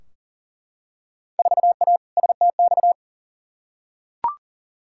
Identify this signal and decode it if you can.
Morse code, heard as a keyed tone, about 40 wpm, 700 Hz. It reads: 4A STX
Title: Field Day Exchanges Encore - 40wpm